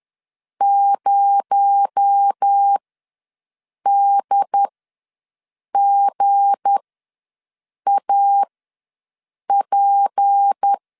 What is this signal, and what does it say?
Morse code, heard as a keyed tone, about 10 wpm, spaced as for 6 wpm, 800 Hz. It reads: 0DGAP
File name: LEZIONI / 7 GF9Z / 7 GF ws 10wpm 3918